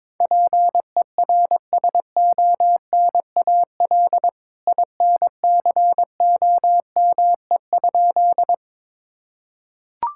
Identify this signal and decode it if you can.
Morse code, heard as a keyed tone, about 22 wpm, 700 Hz. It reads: PERSONAL INCOME?
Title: Sentences from Top 1000 Words - Review Part 5 - 22wpm